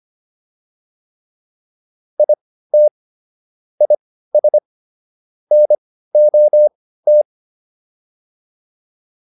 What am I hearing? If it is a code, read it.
Morse code, heard as a keyed tone, about 25 wpm, spaced as for 15 wpm, 600 Hz. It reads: IT IS NOT